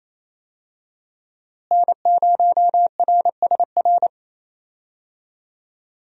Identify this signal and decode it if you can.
Morse code, heard as a keyed tone, about 28 wpm, 700 Hz. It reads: N0RSR
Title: USA Call Signs - 5 Characters - Repeated 2x - 28wpm